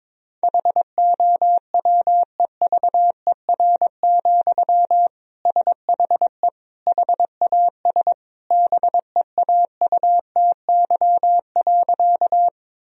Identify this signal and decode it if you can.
Morse code, heard as a keyed tone, about 22 wpm, 700 Hz. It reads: HOWEVER, SHE HAS BEAUTY.
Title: Sentences from Top 1000 Words - Review Part 2 - 22wpm